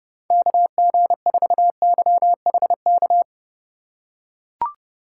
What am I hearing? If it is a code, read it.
Morse code, heard as a keyed tone, about 30 wpm, 700 Hz. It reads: KG4YHK
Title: USA Call Signs - 6 Characters - 3x Character Spacing - 30wpm